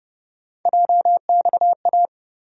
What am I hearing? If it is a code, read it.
Morse code, heard as a keyed tone, about 30 wpm, 700 Hz. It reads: JXA
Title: USA Call Sign Postfixes - 3 Letters 30wpm